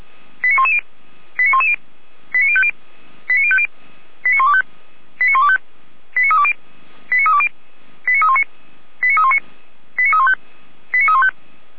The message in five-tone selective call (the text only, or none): 89110 89110 89050 89050 89125 89125 89320 89320 89319 89319 89315 89315